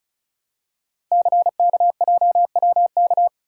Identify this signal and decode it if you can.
Morse code, heard as a keyed tone, about 36 wpm, 690 Hz. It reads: CKJWK
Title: Code Groups - 5 Letters - 35wpm